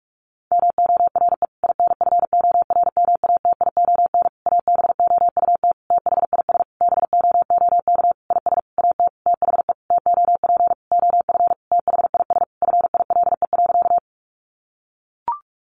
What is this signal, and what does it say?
Morse code, heard as a keyed tone, about 44 wpm, 700 Hz. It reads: MORE INFORMATION ABOUT THIS BOOK IS AT THE TOP OF THIS FILE.
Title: Sentences from Top 1200 Words 45wpm